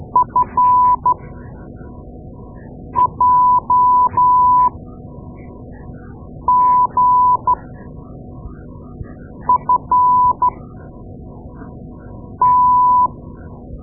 Morse code, heard as a keyed tone, about 13 wpm, spaced as for 4 wpm, 1000 Hz. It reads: FJGFT
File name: Morse Lessons / Morse Lessons / Tape-2A